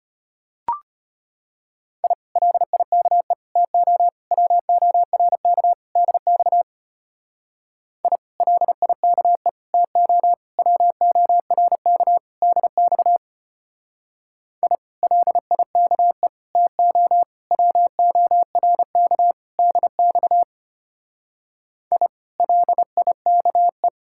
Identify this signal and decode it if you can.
Morse code, heard as a keyed tone, about 29 wpm, 700 Hz. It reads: I LIKE TO WORK DX I LIKE TO WORK DX I LIKE TO WORK DX I LIKE
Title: Speed-Racing Common QSO Phrases Encore - 25wpm